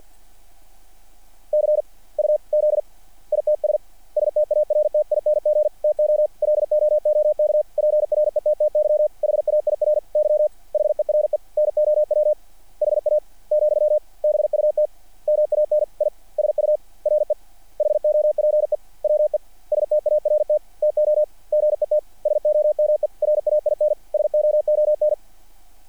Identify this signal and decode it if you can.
Morse code, heard as a keyed tone, about 50 wpm, 600 Hz. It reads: CUZ ITS STARTING TO LOOK PRETTY HAIRY HERE NOW HA , BUT MAN I SURE HOPE WE START TO GET SOME RAIN SOON
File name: archived / CRSnet